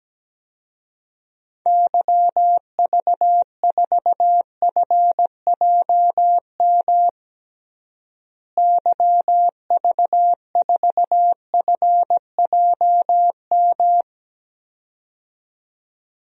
Morse code, heard as a keyed tone, about 17 wpm, 700 Hz. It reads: YV4FJM YV4FJM
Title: DXCC-100 Call Signs - 6 Characters - Repeated 2x - 17wpm